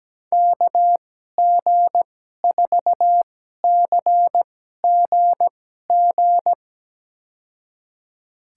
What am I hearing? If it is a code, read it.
Morse code, heard as a keyed tone, about 17 wpm, spaced as for 12 wpm, 700 Hz. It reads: KG4CGG
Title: USA Call Signs - 6 Characters - 2x Character Spacing - 17wpm